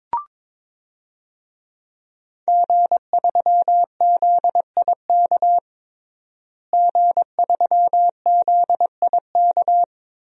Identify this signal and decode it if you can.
Morse code, heard as a keyed tone, about 22 wpm, 700 Hz. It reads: G3ZIK G3ZIK
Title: DXCC-100 Call Signs - 5 Characters - Repeated 2x - 22wpm